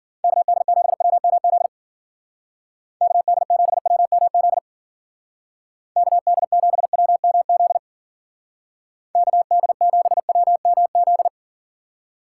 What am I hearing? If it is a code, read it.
Morse code, heard as a keyed tone, about 49 wpm, 700 Hz. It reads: KD7WMZ KD7WMZ KD7WMZ KD7WMZ
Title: Speed Racing - USA 2x3 Call Signs 40wpm